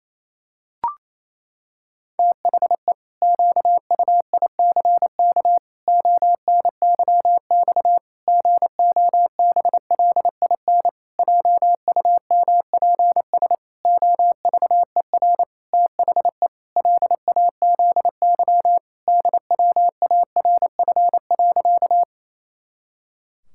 Morse code, heard as a keyed tone, about 28 wpm, 700 Hz. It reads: THE QUICK ONYX GOBLIN JUMPS OVER THE LAZY DWARF.
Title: Pangram Challenge v2 - 28wpm